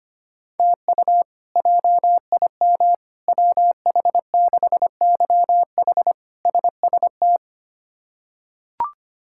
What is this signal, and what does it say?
Morse code, heard as a keyed tone, about 25 wpm, 700 Hz. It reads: TU JIM WH6YH SST
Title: K1USN SST QSO Practice - 25wpm